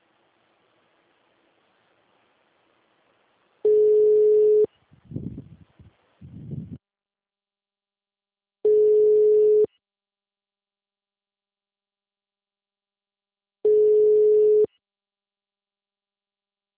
Ringback tone (Europe, 425 Hz).
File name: out- -2004- - - .53606